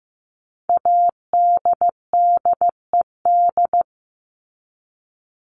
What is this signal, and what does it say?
Morse code, heard as a keyed tone, about 15 wpm, 700 Hz. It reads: ADDED
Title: ED Suffix - Top 100 Words with Base Words 15wpm